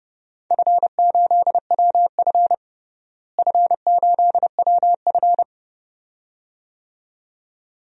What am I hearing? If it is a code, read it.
Morse code, heard as a keyed tone, about 30 wpm, 700 Hz. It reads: F8WF F8WF
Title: DXCC-100 Call Signs - 4 Characters - Repeated 2x - 30wpm